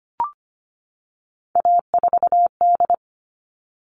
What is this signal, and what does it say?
Morse code, heard as a keyed tone, about 25 wpm, 700 Hz. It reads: A4D
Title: All Cut Numbers - 3-Digit 25wpm